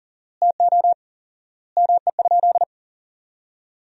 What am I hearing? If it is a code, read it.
Morse code, heard as a keyed tone, about 37 wpm, 700 Hz. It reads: TO ME?
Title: 4x Word-Spacing - Sentences from Top 200 Words - 40wpm